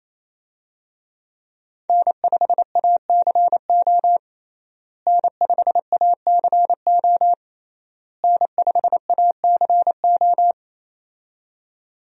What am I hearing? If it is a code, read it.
Morse code, heard as a keyed tone, about 28 wpm, 700 Hz. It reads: N5ACO N5ACO N5ACO
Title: USA Call Signs - 5 Characters - Repeated 3x - 28wpm